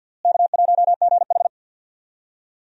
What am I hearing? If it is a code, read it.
Morse code, heard as a keyed tone, about 53 wpm, 700 Hz. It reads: K1GH
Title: USA Call Signs - 4 Characters - Repeated 3x - 50wpm